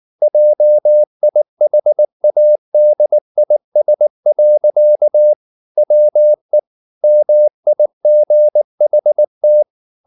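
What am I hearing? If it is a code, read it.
Morse code, heard as a keyed tone, about 19 wpm, 600 Hz. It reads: JIHADIS. WE MIGHT
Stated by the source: Morse code at 19 wpm from CW med Gnister, reading Ken Follett's novel Never in English.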